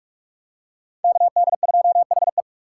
Morse code, heard as a keyed tone, about 46 wpm, 700 Hz. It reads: KD2HE
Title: K1USN SST QSO Practice - 45wpm